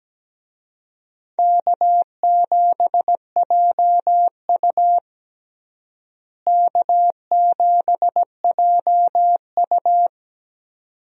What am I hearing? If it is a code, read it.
Morse code, heard as a keyed tone, about 17 wpm, 700 Hz. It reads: K7JU K7JU
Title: USA Call Signs - 4 Characters - Repeated 2x - 17wpm